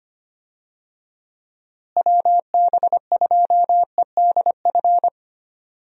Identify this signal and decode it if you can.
Morse code, heard as a keyed tone, about 25 wpm, 700 Hz. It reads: WB2EDF
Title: USA Call Signs - 6 Characters - Repeated 2x - 25wpm